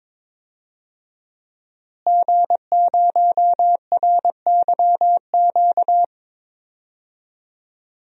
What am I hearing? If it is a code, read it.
Morse code, heard as a keyed tone, about 22 wpm, 700 Hz. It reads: G0RYQ
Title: DXCC-100 Call Signs - 5 Characters - Repeated 3x - 22wpm